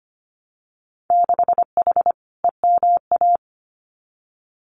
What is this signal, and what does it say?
Morse code, heard as a keyed tone, about 25 wpm, 700 Hz. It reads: 6H EMA
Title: Winter Field Day Exchanges Encore 25wpm